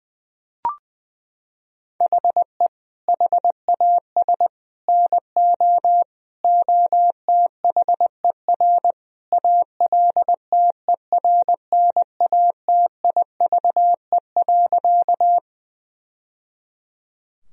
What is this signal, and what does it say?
Morse code, heard as a keyed tone, about 20 wpm, 700 Hz. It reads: HE HAS NO OTHER ALTERNATIVE.
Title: Sentences from Top 1000 Words - Review Part 4 - 20wpm